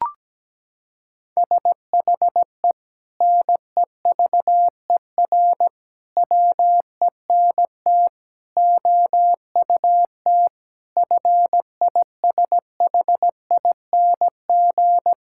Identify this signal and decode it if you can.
Morse code, heard as a keyed tone, about 17 wpm, 700 Hz. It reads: SHE NEVER WENT OUT FISHING
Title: Sentences from Top 1000 Words - Review Part 5 - 17wpm